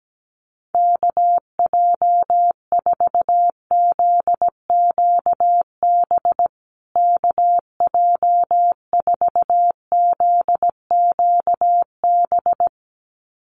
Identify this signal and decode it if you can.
Morse code, heard as a keyed tone, about 17 wpm, 700 Hz. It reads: KJ4ZQB KJ4ZQB
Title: USA Call Signs Repeated 2x 17wpm